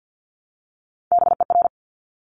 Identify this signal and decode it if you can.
Morse code, heard as a keyed tone, about 50 wpm, 700 Hz. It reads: BER
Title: Top Three Letter Combinations in Words 50wpm